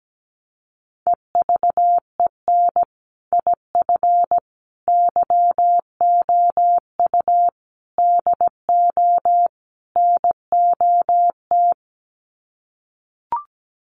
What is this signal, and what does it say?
Morse code, heard as a keyed tone, about 17 wpm, 700 Hz. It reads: EVEN IF YOU DO NOT
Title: Sets of 5 Words Encore 17wpm